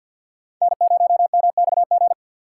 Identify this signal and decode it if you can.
Morse code, heard as a keyed tone, about 52 wpm, 700 Hz. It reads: N0MXG
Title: USA Call Signs - 5 Characters - 2x Character Spacing - 50wpm